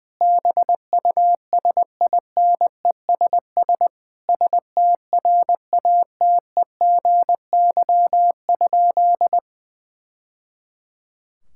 Morse code, heard as a keyed tone, about 20 wpm, 700 Hz. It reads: BUSINESS STRATEGY?
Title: Sentences from Top 1000 Words - Review Part 5 - 20wpm